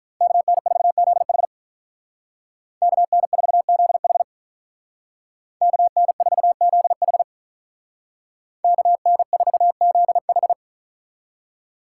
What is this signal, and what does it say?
Morse code, heard as a keyed tone, about 47 wpm, 700 Hz. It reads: KN4ZH KN4ZH KN4ZH KN4ZH